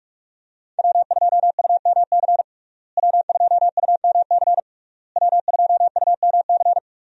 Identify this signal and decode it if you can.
Morse code, heard as a keyed tone, about 41 wpm, 700 Hz. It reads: W2UMC W2UMC W2UMC
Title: USA Call Signs - 5 Characters - Repeated 3x - 45wpm